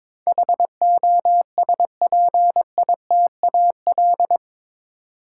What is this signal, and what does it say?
Morse code, heard as a keyed tone, about 22 wpm, 700 Hz. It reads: HOSPITAL